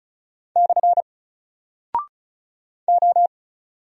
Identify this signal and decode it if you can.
Morse code, heard as a keyed tone, about 36 wpm, 700 Hz. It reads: / O
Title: Single Character (Letter:Number:Punctuation:Prosign) 35wpm